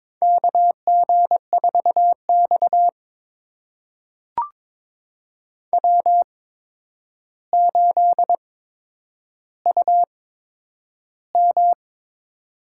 Morse code, heard as a keyed tone, about 22 wpm, spaced as for 9 wpm, 700 Hz. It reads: KG4X W8UM
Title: USA Call Signs - 4 Characters - 8x Character Spacing - 22wpm